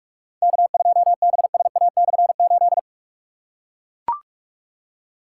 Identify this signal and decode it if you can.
Morse code, heard as a keyed tone, about 41 wpm, 700 Hz. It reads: K2BSA/8
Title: K1USN SST QSO Practice - 45wpm